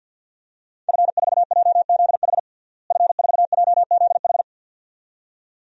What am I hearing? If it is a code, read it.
Morse code, heard as a keyed tone, about 52 wpm, 700 Hz. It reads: F4JZH F4JZH
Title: DXCC-100 Call Signs - 5 Characters - Repeated 2x - 50wpm